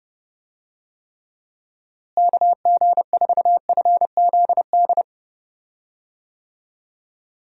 Morse code, heard as a keyed tone, about 30 wpm, 700 Hz. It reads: KG4FZD
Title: USA Call Signs - 6 Characters - Repeated 2x - 30wpm